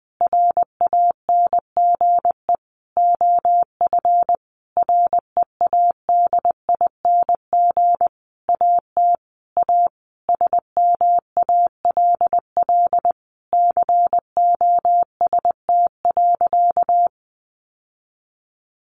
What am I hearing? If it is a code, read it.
Morse code, heard as a keyed tone, about 20 wpm, 700 Hz. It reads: RANGE OF READING AT A SMALL COST.
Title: Sentences from Top 900 Words 20wpm